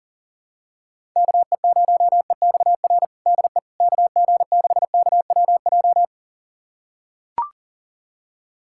Morse code, heard as a keyed tone, about 40 wpm, 700 Hz. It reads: KE0EXR DE KG6KWJ
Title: USA Callsign DE Callsign - 6 Characters - 40wpm